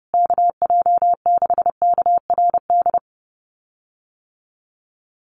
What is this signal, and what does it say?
Morse code, heard as a keyed tone, about 30 wpm, 700 Hz. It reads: KJ6KRD
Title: USA Call Signs Repeated 2x 30wpm